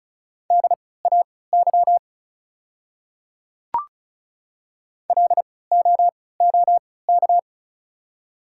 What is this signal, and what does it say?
Morse code, heard as a keyed tone, about 36 wpm, spaced as for 20 wpm, 700 Hz. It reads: DAY LOOK
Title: 3x Character Spacing - Top 100 Words - 35wpm